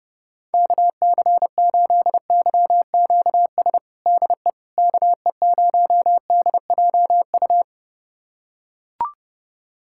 Morse code, heard as a keyed tone, about 31 wpm, 700 Hz. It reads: KC8YQS DE KE0DJU
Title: USA Callsign DE Callsign 30wpm